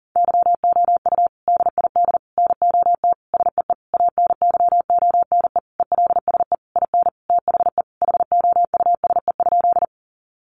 Morse code, heard as a keyed tone, about 40 wpm, 700 Hz. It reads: YOU DID NOT SEE ANYONE ELSE IN THE HOUSE?
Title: Sentences from Top 900 Words 40wpm